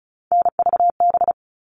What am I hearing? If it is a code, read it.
Morse code, heard as a keyed tone, about 34 wpm, 700 Hz. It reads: NVB
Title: All Cut Numbers - 3-Digit 35wpm